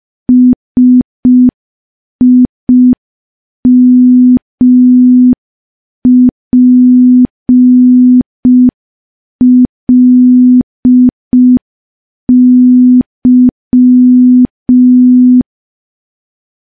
Morse code, heard as a keyed tone, about 5 wpm, 250 Hz. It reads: SIMPLY
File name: morse / morse-training-05WPM